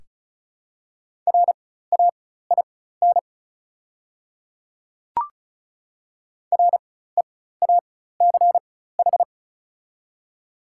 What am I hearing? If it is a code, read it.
Morse code, heard as a keyed tone, about 36 wpm, spaced as for 16 wpm, 700 Hz. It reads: RAIN REACH
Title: 4x Character Spacing - Top 500 Words - 35wpm